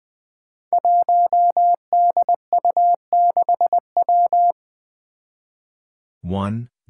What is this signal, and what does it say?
Morse code, heard as a keyed tone, about 20 wpm, 700 Hz. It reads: 1DU6W
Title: Five Challenging Letter-Number Combinations 20wpm